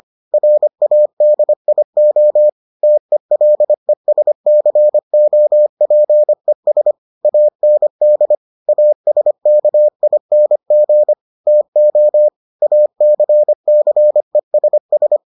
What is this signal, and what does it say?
Morse code, heard as a keyed tone, about 25 wpm, 600 Hz. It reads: RADIO TELESCOPES AND ASKING TO ACCESS